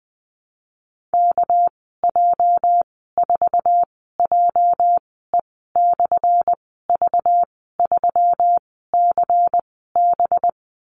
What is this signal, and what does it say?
Morse code, heard as a keyed tone, about 20 wpm, spaced as for 14 wpm, 700 Hz. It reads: KJ4JE/V3CB
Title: 219 Lesson 35g - Number 0 - Call Signs 20wpm